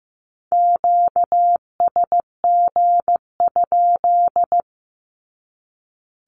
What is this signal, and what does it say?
Morse code, heard as a keyed tone, about 15 wpm, 700 Hz. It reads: QSG?
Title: Q-Codes 15wpm